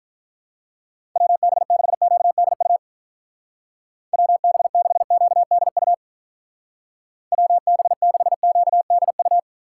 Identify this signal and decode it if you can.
Morse code, heard as a keyed tone, about 46 wpm, 700 Hz. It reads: WB6QDU WB6QDU WB6QDU